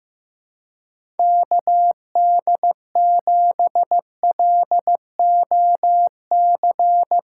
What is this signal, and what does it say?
Morse code, heard as a keyed tone, about 15 wpm, 700 Hz. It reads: KD7LOC
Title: USA Call Signs - 6 Characters - 3x Character Spacing - 15wpm